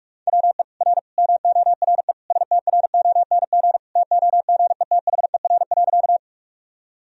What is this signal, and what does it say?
Morse code, heard as a keyed tone, about 46 wpm, 690 Hz. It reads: WE R MORE STRONG TOGETHER.
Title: Sentences from Top 500 Words and Abbreviations 45wpm